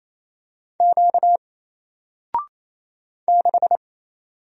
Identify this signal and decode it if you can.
Morse code, heard as a keyed tone, about 28 wpm, 700 Hz. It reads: Q 6
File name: Single-Letter-Number-28wpm